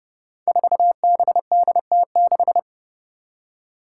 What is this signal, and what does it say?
Morse code, heard as a keyed tone, about 30 wpm, 700 Hz. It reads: 4BDT6
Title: Five Challenging Letter-Number Combinations 30wpm